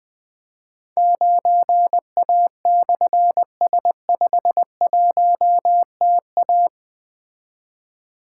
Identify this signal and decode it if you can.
Morse code, heard as a keyed tone, about 20 wpm, 700 Hz. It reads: 9A/S51TA
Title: Difficult Call Signs 20wpm